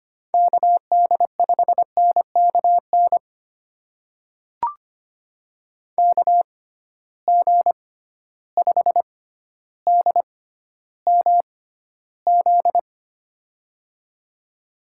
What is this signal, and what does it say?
Morse code, heard as a keyed tone, about 25 wpm, spaced as for 13 wpm, 700 Hz. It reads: KD5NKN KG5DMZ
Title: USA Call Signs - 6 Characters - 6x Character Spacing - 25wpm